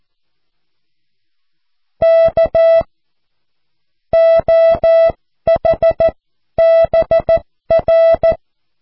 Morse code, heard as a keyed tone, about 14 wpm, spaced as for 12 wpm, 660 Hz. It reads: K OH BR
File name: cw / QSO